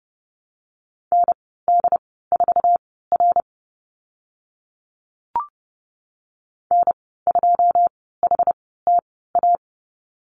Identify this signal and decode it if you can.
Morse code, heard as a keyed tone, about 30 wpm, spaced as for 17 wpm, 700 Hz. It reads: ND4R N2HTA
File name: Lesson 14g - Number 2 - Call Signs 30wpm